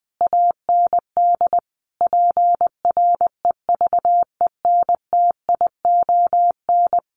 Morse code, heard as a keyed tone, about 20 wpm, 700 Hz. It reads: AND PREVENTION
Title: Sets of 5 Words Encore2 20wpm